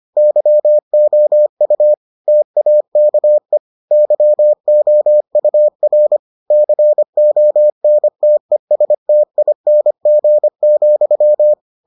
Morse code, heard as a keyed tone, about 25 wpm, 600 Hz. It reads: YOU TAKE YOUR CONTESTING,